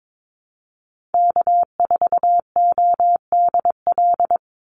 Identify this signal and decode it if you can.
Morse code, heard as a keyed tone, about 22 wpm, 700 Hz. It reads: K4ODL
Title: Call Signs - 22wpm